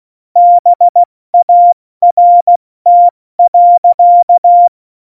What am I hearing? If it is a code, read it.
Morse code, heard as a keyed tone, about 16 wpm, 700 Hz. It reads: BART.
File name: Morsekurs / Witze / Witz 15. Oktober 2019, 70 F80